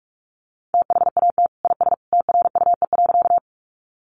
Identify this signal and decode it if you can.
Morse code, heard as a keyed tone, about 44 wpm, 700 Hz. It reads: THAT IS TRUE.
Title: Sentences from Top 400 Words 45wpm